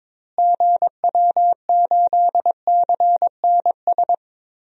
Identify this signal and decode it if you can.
Morse code, heard as a keyed tone, about 22 wpm, 700 Hz. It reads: GW8CNS
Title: DXCC-100 Call Signs - 6 Characters - Repeated 2x - 22wpm